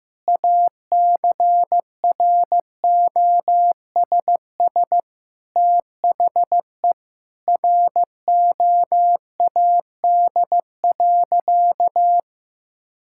Morse code, heard as a keyed tone, about 16 wpm, 700 Hz. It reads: ACROSS THE ROAD.
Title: Sentences from Top 1000 Words - Review Part 5 - 15wpm